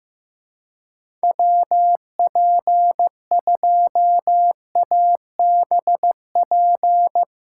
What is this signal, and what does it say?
Morse code, heard as a keyed tone, about 15 wpm, 700 Hz. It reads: WP2ABP
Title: USA Call Signs - 6 Characters - 3x Character Spacing - 15wpm